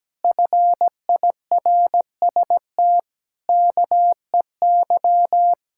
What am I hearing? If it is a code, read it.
Morse code, heard as a keyed tone, about 17 wpm, 700 Hz. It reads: FIRST KEY
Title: Pangram Challenge v2 - 17wpm